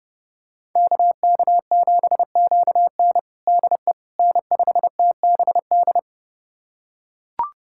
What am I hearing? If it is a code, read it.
Morse code, heard as a keyed tone, about 32 wpm, 700 Hz. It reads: KK7QN DE N5TBD
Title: USA Callsign DE Callsign - 5 Characters - 30wpm